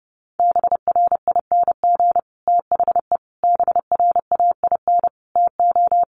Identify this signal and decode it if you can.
Morse code, heard as a keyed tone, about 30 wpm, 700 Hz. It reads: BRING THE BRAIN TO
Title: Sentences from Top 1200 Words 30wpm